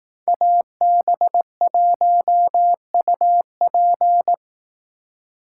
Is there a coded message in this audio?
Yes: Morse code from 0.3 to 4.3 s, heard as a keyed tone, about 19 wpm, 700 Hz. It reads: AB1UP